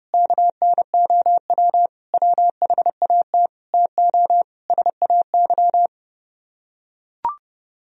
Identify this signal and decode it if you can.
Morse code, heard as a keyed tone, about 30 wpm, 700 Hz. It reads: KNOW WHAT TO SAY